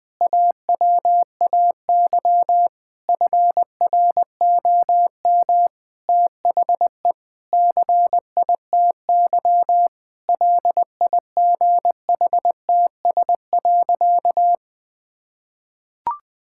Morse code, heard as a keyed tone, about 21 wpm, 700 Hz. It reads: AWAY FROM THE CITY LIGHTS.
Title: Difficult Sentences with Hints - 20wpm